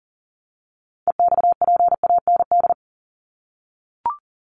Morse code, heard as a keyed tone, about 40 wpm, 700 Hz. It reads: EXPAND
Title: Top 1251-1500 Words 40wpm